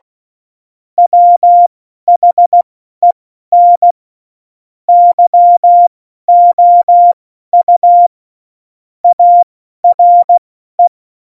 Morse code, heard as a keyed tone, about 16 wpm, spaced as for 12 wpm, 700 Hz. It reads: WHEN YOU ARE
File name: Morsekurs / Witze